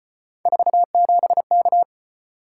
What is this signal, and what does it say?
Morse code, heard as a keyed tone, about 33 wpm, 700 Hz. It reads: 47K